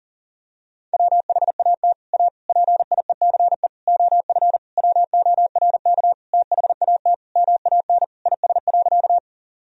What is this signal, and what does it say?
Morse code, heard as a keyed tone, about 40 wpm, 700 Hz. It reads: WHAT A PIECE OF WORK THAT MAN IS.
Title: Sentences from Top 1000 Words - Review Part 2 - 40wpm